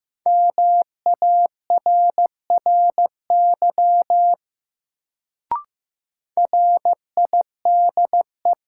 Morse code, heard as a keyed tone, about 15 wpm, 700 Hz. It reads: MARRY RIDE
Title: Sentences from Top 1200 Words - New Word Drill Alt - 15wpm